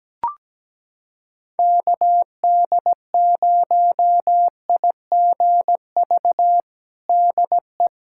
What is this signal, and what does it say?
Morse code, heard as a keyed tone, about 17 wpm, 700 Hz. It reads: KD0IGV DE
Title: USA Callsign DE Callsign - 6 Characters - 17wpm